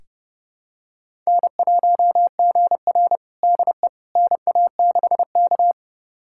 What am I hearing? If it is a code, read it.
Morse code, heard as a keyed tone, about 30 wpm, 700 Hz. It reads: N1GR DE NA6K
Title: USA Callsign DE Callsign - 4 Characters - 30wpm